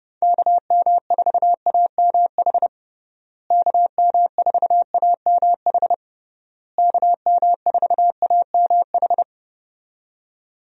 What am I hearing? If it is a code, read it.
Morse code, heard as a keyed tone, about 32 wpm, 700 Hz. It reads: KM4AMH KM4AMH KM4AMH
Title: USA Call Signs - 6 Characters - Repeated 3x - 30wpm